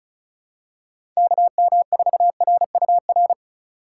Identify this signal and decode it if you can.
Morse code, heard as a keyed tone, about 34 wpm, 700 Hz. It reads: KM4RUR
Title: USA Call Signs - 6 Characters - 2x Character Spacing - 35wpm